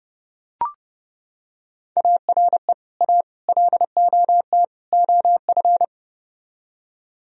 Morse code, heard as a keyed tone, about 30 wpm, 700 Hz. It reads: ARE A LOT OF